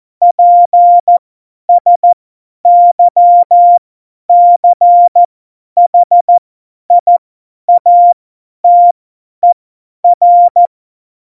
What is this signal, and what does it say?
Morse code, heard as a keyed tone, about 14 wpm, spaced as for 10 wpm, 700 Hz. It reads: PSYCHIATER
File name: Morsekurs / Witze / Witz 09. Januar 2018, 50 F70